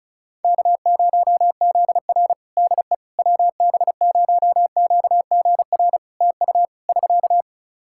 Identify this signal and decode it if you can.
Morse code, heard as a keyed tone, about 36 wpm, 700 Hz. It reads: K0ZR DE WB0QGR TU <SK>
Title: Practice QSO - Exchange 4 - Ending - 35wpm